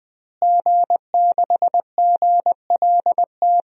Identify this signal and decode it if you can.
Morse code, heard as a keyed tone, about 20 wpm, 700 Hz. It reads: G6GLT
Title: DXCC-100 Call Signs - 5 Characters - Repeated 2x - 20wpm